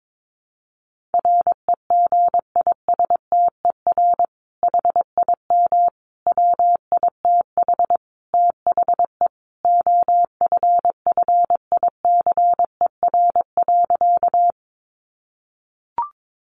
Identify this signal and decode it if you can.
Morse code, heard as a keyed tone, about 22 wpm, 700 Hz. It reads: REGISTER HIM WITH THE OFFICER.
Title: Sentences from Top 1200 Words 22wpm